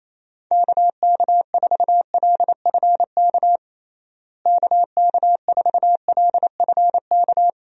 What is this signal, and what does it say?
Morse code, heard as a keyed tone, about 27 wpm, 700 Hz. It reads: KK4LFK KK4LFK
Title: USA Call Signs - 6 Characters - Repeated 3x - 28wpm